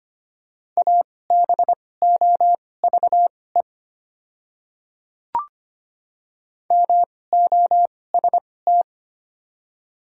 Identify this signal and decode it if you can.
Morse code, heard as a keyed tone, about 25 wpm, spaced as for 18 wpm, 700 Hz. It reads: ABOVE MOST
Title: 2x Character Spacing - Top 500 Words - 25wpm